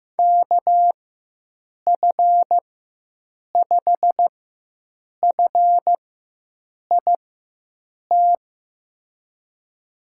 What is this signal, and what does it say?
Morse code, heard as a keyed tone, about 15 wpm, spaced as for 7 wpm, 700 Hz. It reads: KF5FIT
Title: USA Call Signs - 6 Characters - 4x Character Spacing - 15wpm